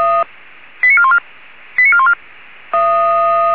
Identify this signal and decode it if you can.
Five-tone selective call (ZVEI), heard as quick runs of five tones, tones of about 80 ms. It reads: 89414 89414